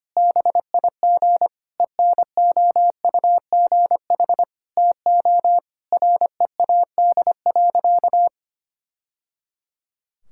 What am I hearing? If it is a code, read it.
Morse code, heard as a keyed tone, about 25 wpm, 700 Hz. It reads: BIG ENOUGH TO READ.
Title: Sentences from Top 1000 Words - Review Part 5 - 25wpm